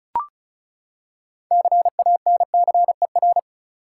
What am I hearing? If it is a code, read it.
Morse code, heard as a keyed tone, about 36 wpm, 700 Hz. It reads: CANCER
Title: Six Letter Words - 35wpm